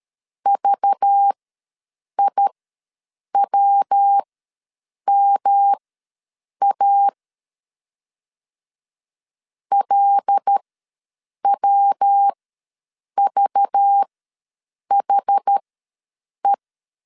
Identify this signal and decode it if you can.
Morse code, heard as a keyed tone, about 13 wpm, spaced as for 7 wpm, 800 Hz. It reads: VIWMA LWVHE
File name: LEZIONI / 5 LBVC / 5 LBV ws 12wpm 3015